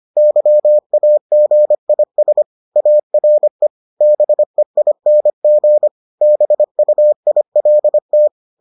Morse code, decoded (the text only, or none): YAGIS ARE BEING BUILT